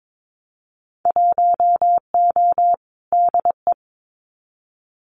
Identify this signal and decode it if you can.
Morse code, heard as a keyed tone, about 22 wpm, 700 Hz. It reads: 1O DE
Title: Winter Field Day Exchanges 22wpm